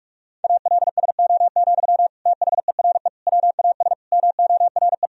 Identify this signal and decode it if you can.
Morse code, heard as a keyed tone, about 46 wpm, 700 Hz. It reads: ALSO, THERE WAS MORE
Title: Sentences from Top 1000 Words - Review Part 1 - 45wpm